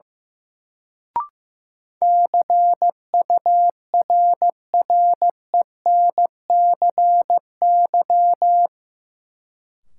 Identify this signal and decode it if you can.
Morse code, heard as a keyed tone, about 16 wpm, 700 Hz. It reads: CURRENCY
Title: Sentences from Top 1500 Words - New Word Drill Alt - 15wpm